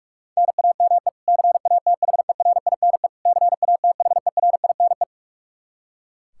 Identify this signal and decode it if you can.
Morse code, heard as a keyed tone, about 46 wpm, 700 Hz. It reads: NAME CATHERINE CATHERINE
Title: Practice QSO - Exchange 1 - Beginning - 45wpm